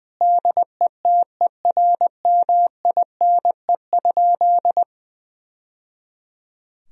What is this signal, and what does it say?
Morse code, heard as a keyed tone, about 20 wpm, 700 Hz. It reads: DETERMINE?
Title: Sentences from Top 1000 Words - Review Part 4 - 20wpm